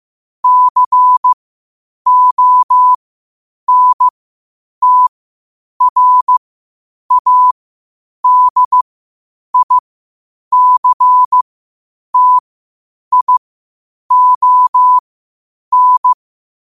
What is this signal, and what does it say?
Morse code, heard as a keyed tone, about 15 wpm, spaced as for 8 wpm, 1000 Hz. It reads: CONTRADICTION
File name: qotdfiles / 8wpm